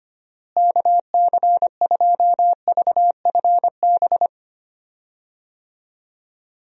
Morse code, heard as a keyed tone, about 25 wpm, 700 Hz. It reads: KC2VFB
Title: USA Call Signs - 6 Characters - Repeated 2x - 25wpm